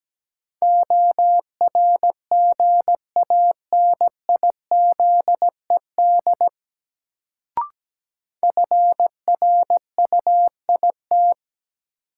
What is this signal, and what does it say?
Morse code, heard as a keyed tone, about 17 wpm, 700 Hz. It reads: ORGANIZED FRUIT
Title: Sentences from Top 1300 Words - New Word Drill - 17wpm